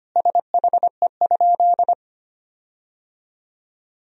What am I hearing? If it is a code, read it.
Morse code, heard as a keyed tone, about 25 wpm, 700 Hz. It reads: SHE?
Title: 4x Word-Spacing - Sentences from Top 200 Words - 25wpm